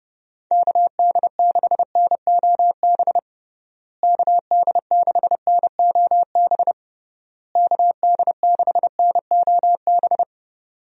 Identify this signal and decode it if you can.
Morse code, heard as a keyed tone, about 30 wpm, 700 Hz. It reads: KD6NOB KD6NOB KD6NOB
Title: USA Call Signs - 6 Characters - Repeated 3x - 30wpm